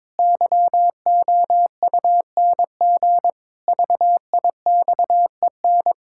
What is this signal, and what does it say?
Morse code, heard as a keyed tone, about 22 wpm, 700 Hz. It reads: YOUNG VIXEN
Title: Pangram Challenge v2 - 22wpm